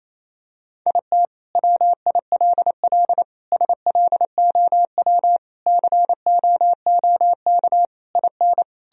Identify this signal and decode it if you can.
Morse code, heard as a keyed tone, about 28 wpm, 700 Hz. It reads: IT WILL SLOW COOK IN
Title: Sentences from Top 1000 Words - Review Part 4 - 28wpm